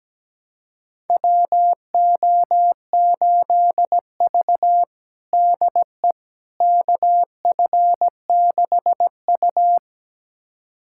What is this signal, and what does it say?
Morse code, heard as a keyed tone, about 17 wpm, 700 Hz. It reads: WO8V DE KF6U
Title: USA Callsign DE Callsign - 4 Characters - 17wpm